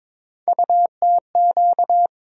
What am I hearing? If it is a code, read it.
Morse code, heard as a keyed tone, about 22 wpm, 700 Hz. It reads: UTQ
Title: USA Call Sign Postfixes - 3 Letters 22wpm